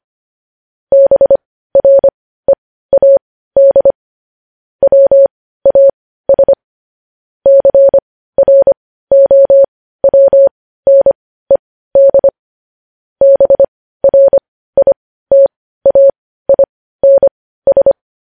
Morse code, heard as a keyed tone, about 25 wpm, spaced as for 15 wpm, 550 Hz. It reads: BREAD WAS CROWNED BRITAINS
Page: News headlines at 15